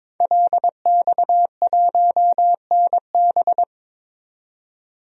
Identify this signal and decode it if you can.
Morse code, heard as a keyed tone, about 21 wpm, 700 Hz. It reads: LX1NB
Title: DXCC-100 Call Signs - 5 Characters - 22wpm